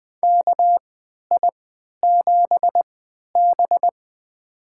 Morse code, heard as a keyed tone, about 20 wpm, spaced as for 11 wpm, 700 Hz. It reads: KI7B
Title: USA Call Signs - 4 Characters - 3x Character Spacing - 20wpm